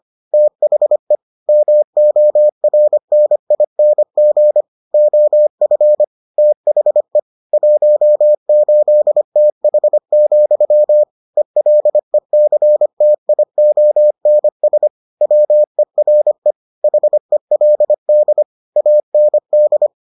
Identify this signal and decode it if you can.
Morse code, heard as a keyed tone, about 25 wpm, 600 Hz. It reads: THE MORNING OF THE 18TH, ELECTIONS WERE HELD AND